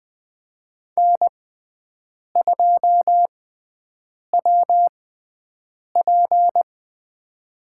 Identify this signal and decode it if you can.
Morse code, heard as a keyed tone, about 21 wpm, spaced as for 7 wpm, 700 Hz. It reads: N2WP
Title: USA Call Signs - 4 Characters - 6x Character Spacing - 20wpm